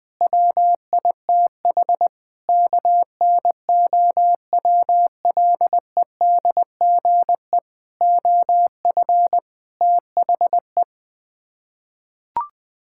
Morse code, heard as a keyed tone, about 20 wpm, 700 Hz. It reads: WITH KNOWLEDGE OF THE
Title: Sets of 8 Words - 20wpm